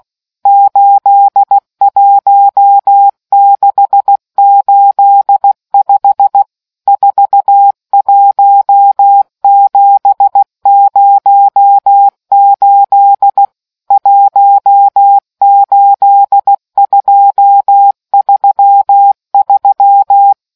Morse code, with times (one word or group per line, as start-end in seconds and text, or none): 0.5-6.4 81685
6.9-13.4 41708
13.9-20.3 18233